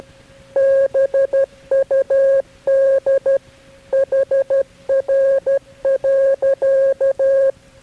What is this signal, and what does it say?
Morse code, heard as a keyed tone, about 12 wpm, 540 Hz. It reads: BUD HR.
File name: KB6MT-tape3-side1